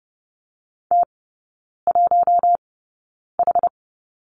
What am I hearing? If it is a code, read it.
Morse code, heard as a keyed tone, about 30 wpm, spaced as for 9 wpm, 700 Hz. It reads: T1H
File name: Lesson 18e - Letter m - Three Characters 30wpm